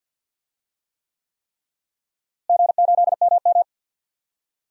Morse code, heard as a keyed tone, about 52 wpm, 700 Hz. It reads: G8MM
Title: DXCC-100 Call Signs - 4 Characters - Repeated 3x - 50wpm